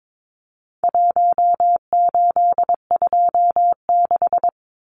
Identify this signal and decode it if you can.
Morse code, heard as a keyed tone, about 22 wpm, 700 Hz. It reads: 1826
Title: Numbers - Contest Style - 4-Digit 22wpm